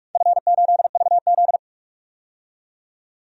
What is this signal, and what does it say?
Morse code, heard as a keyed tone, about 44 wpm, 700 Hz. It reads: F8VZ
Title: DXCC-100 Call Signs - 4 Characters - 45wpm